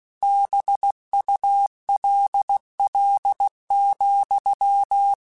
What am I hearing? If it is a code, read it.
Morse code, heard as a keyed tone, about 16 wpm, 800 Hz. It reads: BULL,
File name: Misc / Misc-mp3s / BIG @15